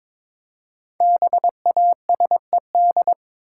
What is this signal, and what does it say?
Morse code, heard as a keyed tone, about 22 wpm, 700 Hz. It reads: BASED
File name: 5 Letter Words - First 3 Letters Given - 22wpm